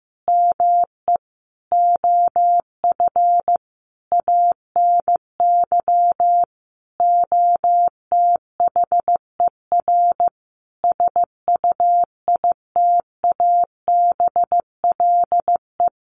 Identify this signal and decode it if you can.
Morse code, heard as a keyed tone, about 15 wpm, 700 Hz. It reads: ME OF ANY OTHER SUITABLE
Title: Sentences from Top 1400 Words 15wpm